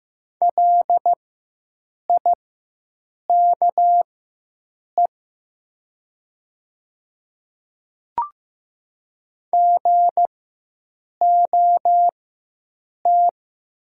Morse code, heard as a keyed tone, about 15 wpm, spaced as for 7 wpm, 700 Hz. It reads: LIKE GOT
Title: 4x Character Spacing - Top 300 Words - 15wpm